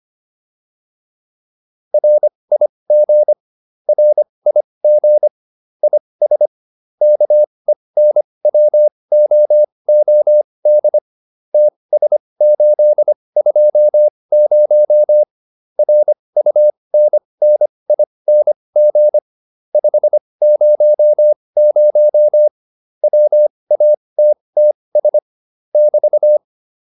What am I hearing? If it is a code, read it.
Morse code, heard as a keyed tone, about 25 wpm, spaced as for 20 wpm, 600 Hz. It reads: RIG RIG IS KENWOOD TS820 RUNNING 500 WATTS <BT>